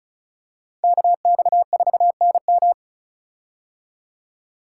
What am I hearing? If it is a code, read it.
Morse code, heard as a keyed tone, about 36 wpm, 700 Hz. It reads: KX4NM
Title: K1USN SST QSO Practice - 35wpm